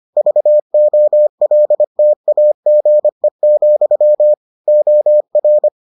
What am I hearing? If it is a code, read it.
Morse code, heard as a keyed tone, about 25 wpm, 600 Hz. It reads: VOLTAGE, OR